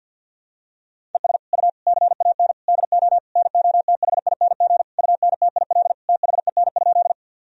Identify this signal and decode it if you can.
Morse code, heard as a keyed tone, about 46 wpm, 700 Hz. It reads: ES U CAN DO NOTHING UNTIL THEN?
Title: Sentences from Top 500 Words and Abbreviations 50wpm